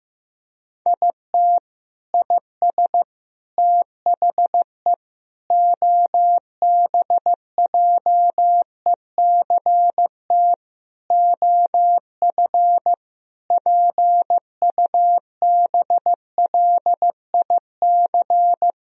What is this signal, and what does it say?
Morse code, heard as a keyed tone, about 15 wpm, 700 Hz. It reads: IT IS THE OBJECT OF PUBLIC
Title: Sentences from Top 1000 Words - Review Part 4 - 15wpm